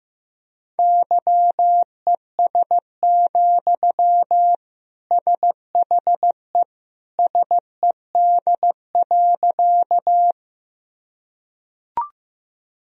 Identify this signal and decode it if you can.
Morse code, heard as a keyed tone, about 16 wpm, 700 Hz. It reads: YES, SHE SED.
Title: Sentences from Top 500 Words and Abbreviations 15wpm